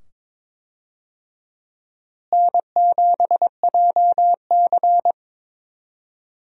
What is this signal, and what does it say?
Morse code, heard as a keyed tone, about 22 wpm, 700 Hz. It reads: N7JC
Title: USA Call Signs - 4 Characters - Repeated 2x - 22wpm